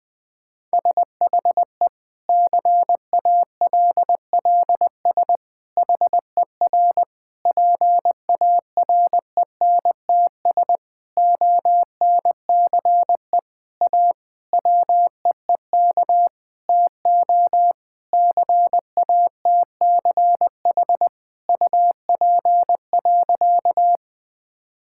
Morse code, heard as a keyed tone, about 20 wpm, 700 Hz. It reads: SHE CALLS HER PARENTS ONCE A WEEK TO CATCH UP.
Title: Difficult Sentences with Hints - 20wpm